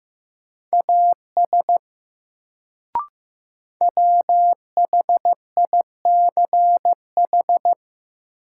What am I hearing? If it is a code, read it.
Morse code, heard as a keyed tone, about 15 wpm, 700 Hz. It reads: AS WHICH
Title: Sets of 2 Words - New Word Drill - 15wpm